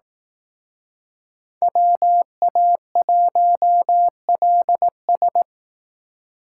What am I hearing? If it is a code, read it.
Morse code, heard as a keyed tone, about 18 wpm, 700 Hz. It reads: WA1LS
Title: Speed Racing - USA 2x2 Call Signs 15wpm